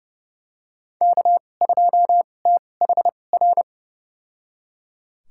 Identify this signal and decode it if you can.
Morse code, heard as a keyed tone, about 31 wpm, spaced as for 22 wpm, 700 Hz. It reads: K2THR
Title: USA Call Signs - 2x Character Spacing - 30wpm